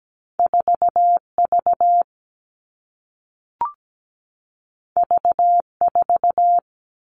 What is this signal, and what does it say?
Morse code, heard as a keyed tone, about 17 wpm, 700 Hz. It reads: 4V V4
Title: All Cut Numbers - 2-Digit 17wpm